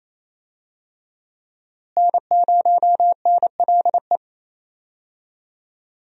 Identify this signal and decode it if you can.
Morse code, heard as a keyed tone, about 28 wpm, 700 Hz. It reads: N0NLE
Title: USA Call Signs - 5 Characters - Repeated 3x - 28wpm